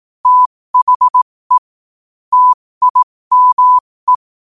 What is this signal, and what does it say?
Morse code, heard as a keyed tone, about 18 wpm, spaced as for 15 wpm, 1000 Hz. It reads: THE TIME
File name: qotdfiles / 15wpm